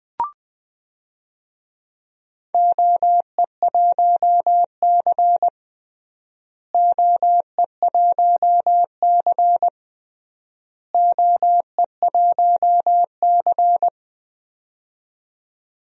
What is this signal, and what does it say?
Morse code, heard as a keyed tone, about 20 wpm, 700 Hz. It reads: OE1C OE1C OE1C
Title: DXCC-100 Call Signs - 4 Characters - Repeated 3x - 20wpm